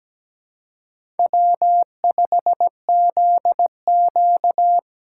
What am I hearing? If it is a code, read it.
Morse code, heard as a keyed tone, about 17 wpm, 700 Hz. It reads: W5ZQ
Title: USA Call Signs - 4 Characters - 4x Character Spacing - 17wpm